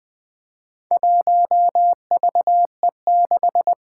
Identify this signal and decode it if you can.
Morse code, heard as a keyed tone, about 20 wpm, 700 Hz. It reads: 1VE6
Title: Four Challenging Letter-Number Combinations 20wpm